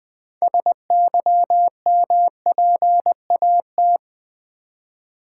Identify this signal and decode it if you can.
Morse code, heard as a keyed tone, about 20 wpm, 700 Hz. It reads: SYMPAT
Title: Build A Word - 3 Letters At A Time - 8-Letter Words - 20wpm